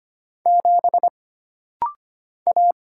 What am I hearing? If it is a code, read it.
Morse code, heard as a keyed tone, about 25 wpm, 700 Hz. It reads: 7 A
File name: Single Character (Letter:Number:Punctuation:Prosign) - Rapid-Fire - 25wpm